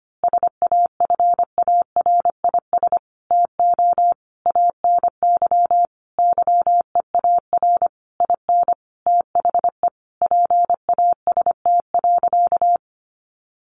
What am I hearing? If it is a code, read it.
Morse code, heard as a keyed tone, about 25 wpm, 700 Hz. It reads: SAFARIS TO ANY YEAR IN THE PAST.
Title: A Sound Of Thunder 25wpm